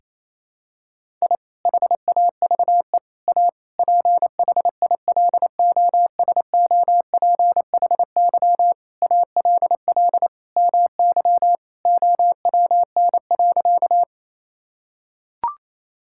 Morse code, heard as a keyed tone, about 28 wpm, 700 Hz. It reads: I HAVE A PHILOSOPHY ALL MY OWN.
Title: Sentences from Top 1000 Words - Review Part 6 - 28wpm